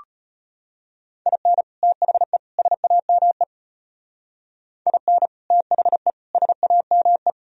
Morse code, heard as a keyed tone, about 33 wpm, 700 Hz. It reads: IN THE SAME IN THE SAME